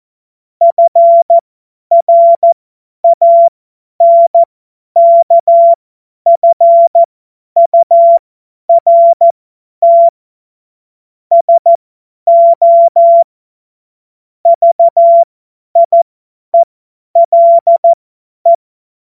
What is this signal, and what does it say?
Morse code, heard as a keyed tone, about 14 wpm, spaced as for 10 wpm, 680 Hz. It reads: FRANKFURT SO VIELE
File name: Morsekurs / Witze / Witz 03 Juni 2025 50 F70